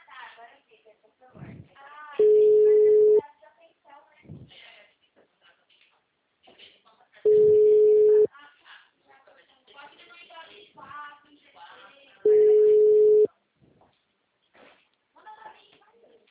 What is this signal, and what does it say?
Ringback tone (Europe, 425 Hz), with speech.
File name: out- -2015- - - .71